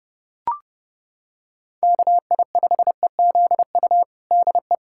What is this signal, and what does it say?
Morse code, heard as a keyed tone, about 32 wpm, 700 Hz. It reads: KI5EZU DE
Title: USA Callsign DE Callsign - 6 Characters - 30wpm